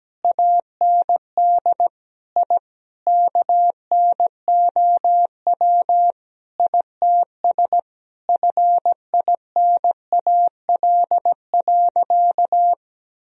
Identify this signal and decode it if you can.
Morse code, heard as a keyed tone, about 17 wpm, 700 Hz. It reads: AND I KNOW ITS FINAL.
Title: Sentences from Top 1000 Words - Review Part 2 - 17wpm